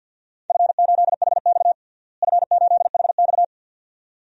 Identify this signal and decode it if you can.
Morse code, heard as a keyed tone, about 51 wpm, 700 Hz. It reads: F8HX F8HX
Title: DXCC-100 Call Signs - 4 Characters - Repeated 2x - 50wpm